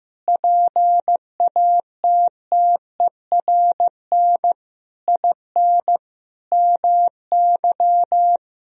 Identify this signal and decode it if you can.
Morse code, heard as a keyed tone, about 15 wpm, 700 Hz. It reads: PATTERN IN MY
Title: Sentences from Top 1000 Words - Review Part 2 - 15wpm